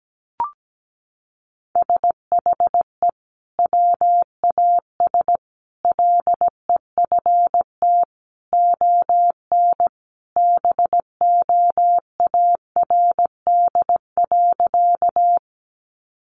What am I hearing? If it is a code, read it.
Morse code, heard as a keyed tone, about 17 wpm, 700 Hz. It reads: SHE WAS LEFT ON BOARD.
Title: Sentences from Top 700 Words 17wpm